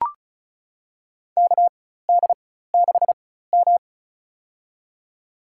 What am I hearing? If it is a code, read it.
Morse code, heard as a keyed tone, about 36 wpm, spaced as for 16 wpm, 700 Hz. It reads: KD6M
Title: USA Call Signs - 4 Characters - 4x Character Spacing - 35wpm